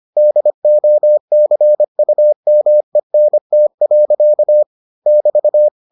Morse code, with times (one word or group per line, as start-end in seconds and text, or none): 0.2-4.6 DOCUMENT.
5.1-5.7 <BT>